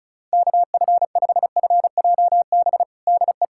Morse code, heard as a keyed tone, about 36 wpm, 700 Hz. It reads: KF5FJB DE
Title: USA Callsign DE Callsign - 6 Characters - 35wpm